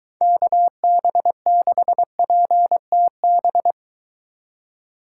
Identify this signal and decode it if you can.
Morse code, heard as a keyed tone, about 23 wpm, 700 Hz. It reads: KB6PTB